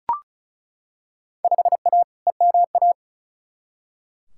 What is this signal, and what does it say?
Morse code, heard as a keyed tone, about 36 wpm, 690 Hz. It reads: 5A EMA
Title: Field Day Exchanges Encore - 35wpm